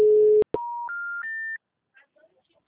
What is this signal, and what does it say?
Special information tone.
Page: Rio 3 - Ura